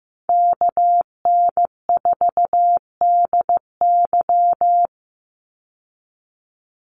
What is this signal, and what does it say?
Morse code, heard as a keyed tone, about 15 wpm, 700 Hz. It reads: KN4DY
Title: USA Call Signs 15wpm